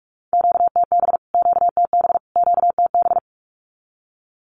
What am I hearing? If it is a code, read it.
Morse code, heard as a keyed tone, about 44 wpm, 700 Hz. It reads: QTB QTB QTB
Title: Q-Codes Repeated 3x 45wpm